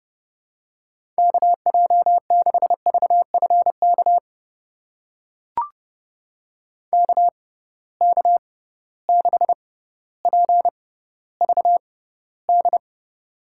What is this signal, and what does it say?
Morse code, heard as a keyed tone, about 31 wpm, spaced as for 15 wpm, 700 Hz. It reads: KJ6VFK KK6PVD
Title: USA Call Signs - 6 Characters - 6x Character Spacing - 30wpm